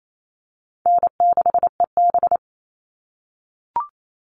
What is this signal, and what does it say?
Morse code, heard as a keyed tone, about 28 wpm, 700 Hz. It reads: N6EB
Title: All Cut Numbers - 4-Digit 28wpm